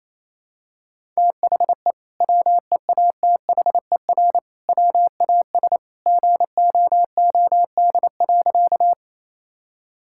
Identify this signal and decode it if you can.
Morse code, heard as a keyed tone, about 28 wpm, 700 Hz. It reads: THE WEATHER WAS GOOD.
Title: Sentences from Top 1000 Words - Review Part 5 - 28wpm